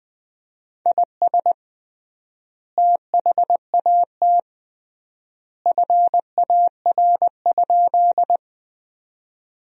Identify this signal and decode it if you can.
Morse code, heard as a keyed tone, about 20 wpm, 700 Hz. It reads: IS THAT FAR?
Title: 3x Word-Spacing - Sentences from Top 300 Words - 20wpm